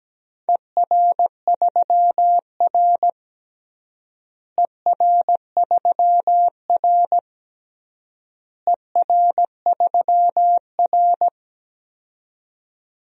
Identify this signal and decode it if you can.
Morse code, heard as a keyed tone, about 17 wpm, 700 Hz. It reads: ER3R ER3R ER3R
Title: DXCC-100 Call Signs - 4 Characters - Repeated 3x - 17wpm